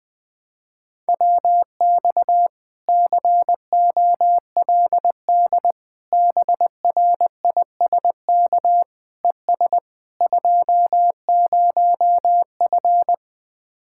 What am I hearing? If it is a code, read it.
Morse code, heard as a keyed tone, about 20 wpm, 700 Hz. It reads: WX COLD BRISK ES 20F
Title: Speed-Racing Common QSO Phrases Encore - 20wpm